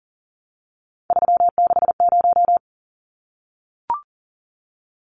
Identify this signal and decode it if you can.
Morse code, heard as a keyed tone, about 40 wpm, 700 Hz. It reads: 360
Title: Numbers - 3-Digit 40wpm